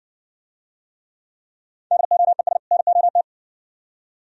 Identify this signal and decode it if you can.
Morse code, heard as a keyed tone, about 60 wpm, 700 Hz. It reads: DOES NOT